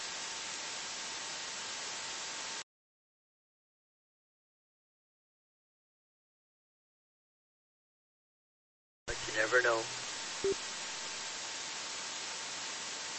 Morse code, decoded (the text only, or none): none